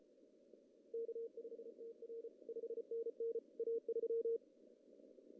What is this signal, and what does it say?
Morse code, heard as a keyed tone, about 34 wpm, 470 Hz. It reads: K5TR 5NN A3
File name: CW / cqww-cw-k5tr